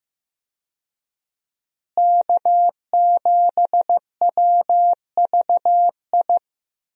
Morse code, heard as a keyed tone, about 15 wpm, 700 Hz. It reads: K7WVI
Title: USA Call Signs - 5 Characters - Repeated 2x - 15wpm